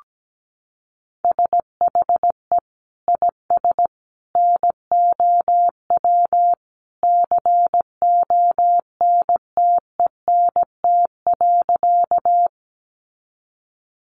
Morse code, heard as a keyed tone, about 17 wpm, 700 Hz. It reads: SHE IS NOW CONTENT.
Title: Sentences from Top 900 Words 17wpm